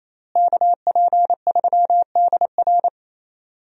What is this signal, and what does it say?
Morse code, heard as a keyed tone, about 28 wpm, 700 Hz. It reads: KP3DR
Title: USA Call Signs - 5 Characters - Repeated 2x - 28wpm